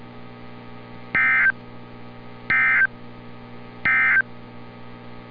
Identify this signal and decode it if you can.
Emergency Alert System (SAME) tones, heard as harsh buzzing bursts. It is heard from 1.0 to 4.3 s.
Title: NOAA Weather Radio - Altoona, PA